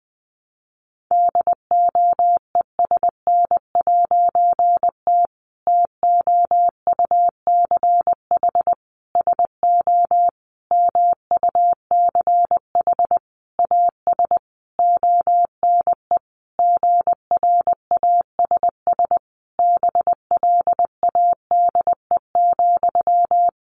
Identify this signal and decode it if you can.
Morse code, heard as a keyed tone, about 20 wpm, 700 Hz. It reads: DOESN'T TOUCH SO MUCH AS ONE GRASS BLADE,
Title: A Sound Of Thunder 20wpm